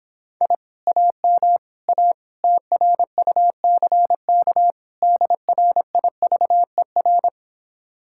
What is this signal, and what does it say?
Morse code, heard as a keyed tone, about 26 wpm, 700 Hz. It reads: I AM A TRUCK DRIVER